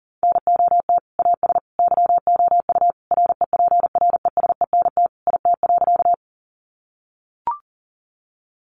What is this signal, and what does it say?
Morse code, heard as a keyed tone, about 40 wpm, 700 Hz. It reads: NOT AS YOU REPRESENT IT.
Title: Sentences from Top 1200 Words 40wpm